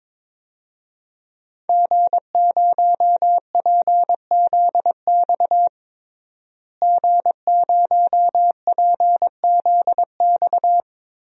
Morse code, heard as a keyed tone, about 22 wpm, 700 Hz. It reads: G0PZX G0PZX
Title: DXCC-100 Call Signs - 5 Characters - Repeated 3x - 22wpm